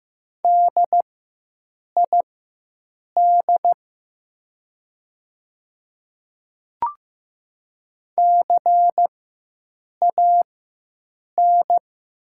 Morse code, heard as a keyed tone, about 15 wpm, spaced as for 7 wpm, 700 Hz. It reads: DID CAN
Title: 4x Character Spacing - Top 300 Words - 15wpm